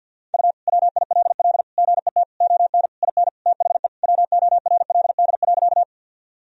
Morse code, heard as a keyed tone, about 46 wpm, 700 Hz. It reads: U WILL GET ON IN THE WORLD.
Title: Sentences from Top 200 Words and Abbreviations 50wpm